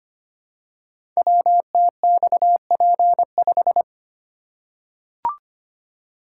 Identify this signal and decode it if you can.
Morse code, heard as a keyed tone, about 25 wpm, 700 Hz. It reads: WTXP5
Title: Code Groups - 5 Letter:Numbers - 25wpm